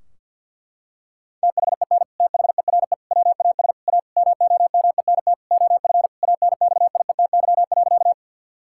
Morse code, heard as a keyed tone, about 49 wpm, 700 Hz. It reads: THEN THERE WAS A MOMENT OF ANXIETY.
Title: Sentences from Top 1600 Words 50wpm